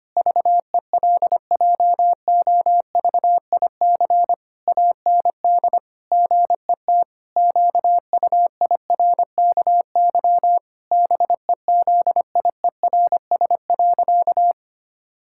Morse code, heard as a keyed tone, about 25 wpm, 700 Hz. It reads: VELJOVIC AND GET QUIRKY BEZIERS.
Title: Pangram Challenge v2 - 25wpm